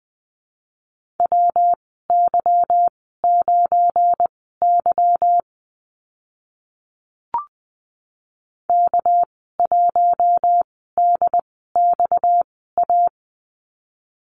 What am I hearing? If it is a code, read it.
Morse code, heard as a keyed tone, about 20 wpm, spaced as for 15 wpm, 700 Hz. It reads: WY9Y K1DXA
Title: 231 Lesson 37g - Letter x - Call Signs 20wpm